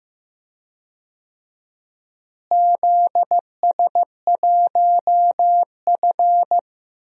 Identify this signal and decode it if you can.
Morse code, heard as a keyed tone, about 15 wpm, 700 Hz. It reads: ZS1F
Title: DXCC-100 Call Signs - 4 Characters - Repeated 3x - 15wpm